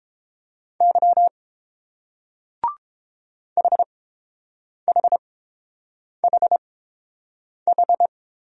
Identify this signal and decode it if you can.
Morse code, heard as a keyed tone, about 28 wpm, spaced as for 7 wpm, 690 Hz. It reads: Y HHHH